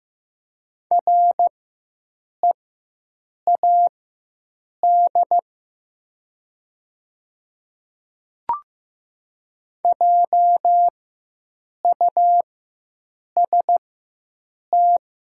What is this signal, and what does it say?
Morse code, heard as a keyed tone, about 15 wpm, spaced as for 7 wpm, 700 Hz. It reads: READ JUST
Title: 4x Character Spacing - Top 200 Words - 15wpm